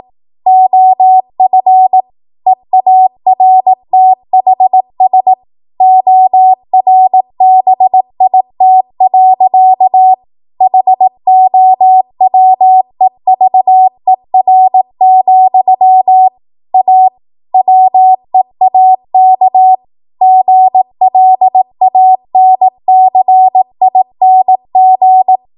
Morse code, heard as a keyed tone, about 18 wpm, 750 Hz. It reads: OF EARTHS ORBIT. HOWEVER, A WEAK GLANCING